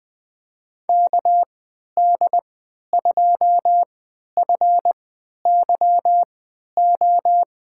Morse code, heard as a keyed tone, about 21 wpm, spaced as for 11 wpm, 700 Hz. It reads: KD2FYO
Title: USA Call Signs - 6 Characters - 3x Character Spacing - 20wpm